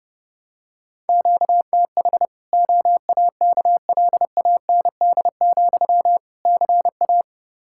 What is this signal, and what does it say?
Morse code, heard as a keyed tone, about 32 wpm, 700 Hz. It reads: QTH OAKLAND, CA
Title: K1USN SST QSO Practice 30wpm